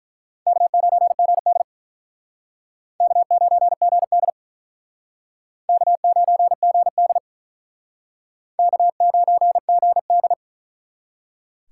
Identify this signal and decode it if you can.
Morse code, heard as a keyed tone, about 43 wpm, 700 Hz. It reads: K9GD K9GD K9GD K9GD
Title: Speed Racing - USA 1x2 Call Signs 35wpm